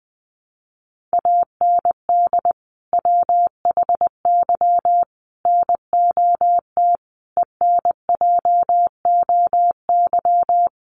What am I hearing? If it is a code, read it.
Morse code, heard as a keyed tone, about 20 wpm, 700 Hz. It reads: AND WHY NOT ENJOY
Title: Sentences from Top 1300 Words 20wpm